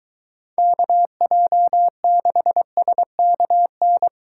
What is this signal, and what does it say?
Morse code, heard as a keyed tone, about 23 wpm, 700 Hz. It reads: KJ6SKN